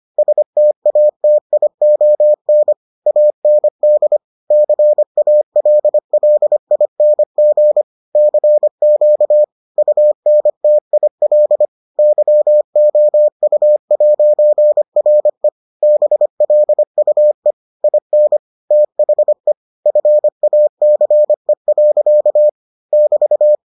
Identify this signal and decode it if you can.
Morse code, heard as a keyed tone, about 25 wpm, 600 Hz. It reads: STATION AND CALLING CQ UNTIL YOU'RE BLUE IN THE FACE. <BT>